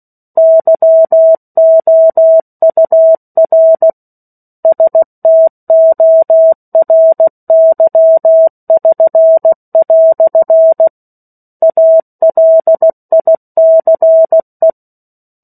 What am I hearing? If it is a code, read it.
Morse code, heard as a keyed tone, about 16 wpm, 650 Hz. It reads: YOUR STORY<SN>" ALICE
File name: 03-Chapter-03 16 WPM